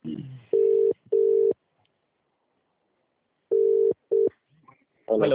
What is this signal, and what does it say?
Ringback tone (UK), 400 + 450 Hz.